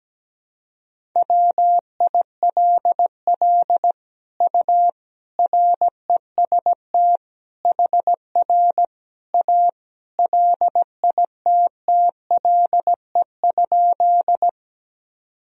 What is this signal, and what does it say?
Morse code, heard as a keyed tone, about 17 wpm, 700 Hz. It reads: WILL U REST HR A LITTLE?
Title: Sentences from Top 500 Words and Abbreviations 17wpm